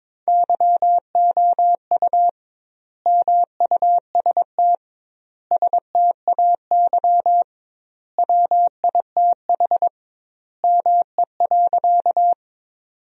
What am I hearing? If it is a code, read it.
Morse code, heard as a keyed tone, about 22 wpm, 700 Hz. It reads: YOU MUST STAY WITH ME.
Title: 2x Word-Spacing - Sentences from Top 500 Words - 22wpm